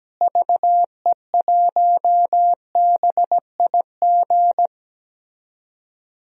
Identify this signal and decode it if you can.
Morse code, heard as a keyed tone, about 17 wpm, 700 Hz. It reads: VE1BIG
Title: DXCC-100 Call Signs - 6 Characters - Repeated 3x - 17wpm